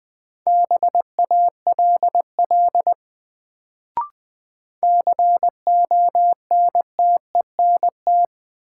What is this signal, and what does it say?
Morse code, heard as a keyed tone, about 20 wpm, 700 Hz. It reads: BALL CONTENT
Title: Sentences from Top 900 Words - New Word Drill - 20wpm